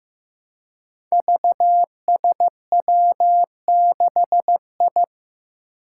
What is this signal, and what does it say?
Morse code, heard as a keyed tone, about 15 wpm, 700 Hz. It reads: VSW6I
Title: Five Challenging Letter-Number Combinations 15wpm